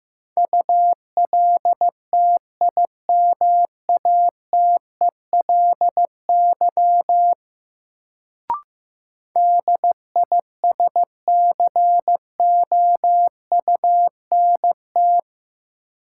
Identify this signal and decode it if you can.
Morse code, heard as a keyed tone, about 15 wpm, 700 Hz. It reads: ULTIMATELY DISCOUNT
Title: Sentences from Top 1200 Words - New Word Drill Alt - 15wpm